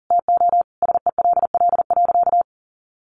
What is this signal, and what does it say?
Morse code, heard as a keyed tone, about 40 wpm, 700 Hz. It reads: TO SELL.
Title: Sentences from Top 1100 Words 40wpm